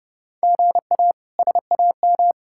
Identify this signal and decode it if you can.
Morse code, heard as a keyed tone, about 32 wpm, 700 Hz. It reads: GA SAM
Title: K1USN SST QSO Practice - 30wpm